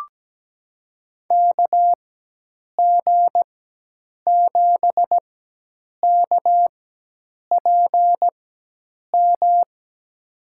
Morse code, heard as a keyed tone, about 17 wpm, spaced as for 8 wpm, 700 Hz. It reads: KG7KPM
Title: USA Call Signs - 6 Characters - 4x Character Spacing - 17wpm